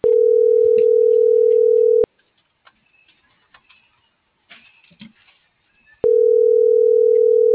Ringback tone (North America), 440 + 480 Hz.